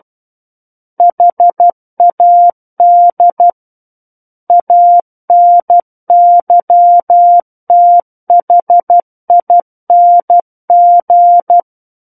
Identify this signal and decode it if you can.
Morse code, heard as a keyed tone, about 12 wpm, 700 Hz. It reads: HAD ANYTHING
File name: Book One / 03-Chapter 3 - 12 WPM